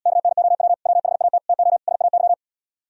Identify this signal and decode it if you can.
Morse code, heard as a keyed tone, about 78 wpm, 700 Hz. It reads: KNOW WHAT TO SAY